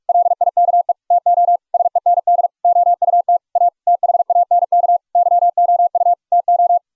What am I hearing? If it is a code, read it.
Morse code, heard as a keyed tone, about 41 wpm, 700 Hz. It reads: LIKE TO SEND OUT A THANK YOU TO